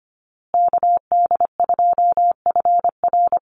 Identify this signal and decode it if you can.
Morse code, heard as a keyed tone, about 25 wpm, 700 Hz. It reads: KD2FR
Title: USA Call Signs Repeated 2x 25wpm